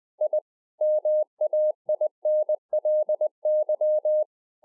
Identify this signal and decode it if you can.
Morse code, heard as a keyed tone, about 20 wpm, 610 Hz. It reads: I MAINLY